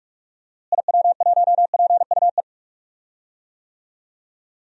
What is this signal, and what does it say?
Morse code, heard as a keyed tone, about 46 wpm, 700 Hz. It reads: IW1PUE
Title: DXCC-100 Call Signs - 6 Characters - Repeated 3x - 45wpm